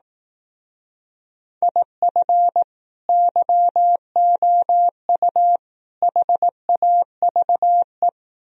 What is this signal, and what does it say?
Morse code, heard as a keyed tone, about 18 wpm, 700 Hz. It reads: IF YOU HAVE
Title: Speed-Racing - Sets of 3 Words 15wpm